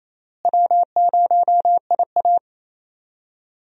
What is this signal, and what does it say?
Morse code, heard as a keyed tone, about 28 wpm, 700 Hz. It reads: W0IA
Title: USA Call Signs - 4 Characters - Repeated 3x - 28wpm